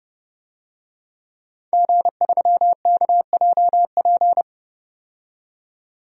Morse code, heard as a keyed tone, about 30 wpm, 700 Hz. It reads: G3KJP
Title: DXCC-100 Call Signs - 5 Characters - 30wpm